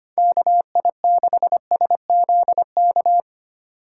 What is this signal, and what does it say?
Morse code, heard as a keyed tone, about 25 wpm, 700 Hz. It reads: KI6SZK
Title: USA Call Signs - 6 Characters - 2x Character Spacing - 25wpm